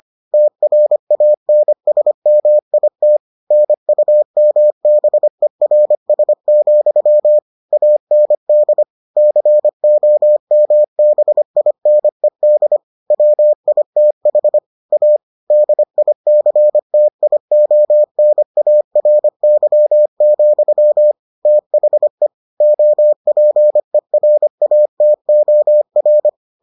Morse code, heard as a keyed tone, about 25 wpm, 600 Hz. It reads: TRANSMIT NUMBERS, AND COMBINED WITH A DICTIONARY, THE OPERATOR